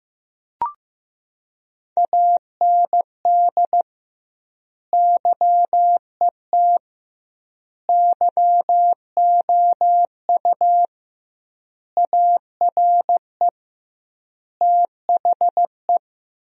Morse code, heard as a keyed tone, about 15 wpm, 700 Hz. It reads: AND YET YOU ARE THE
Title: 2x Word-Spacing - Sentences from Top 500 Words - 15wpm